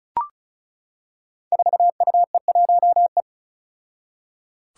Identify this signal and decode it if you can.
Morse code, heard as a keyed tone, about 34 wpm, 700 Hz. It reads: 4UE1E
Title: Five Challenging Letter-Number Combinations 35wpm